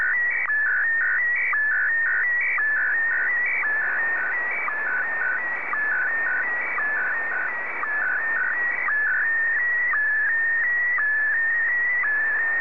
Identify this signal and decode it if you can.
SSTV (slow-scan television), Scottie DX.
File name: sstv / SDX-VK2GHO